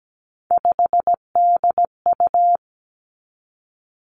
Morse code, heard as a keyed tone, about 17 wpm, 700 Hz. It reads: 5DU
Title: Three Challenging Letter-Number Combinations 17wpm